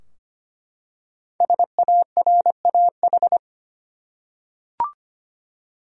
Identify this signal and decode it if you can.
Morse code, heard as a keyed tone, about 25 wpm, 700 Hz. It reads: SARAH
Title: USA Common Names - Max 5 Letters - 25wpm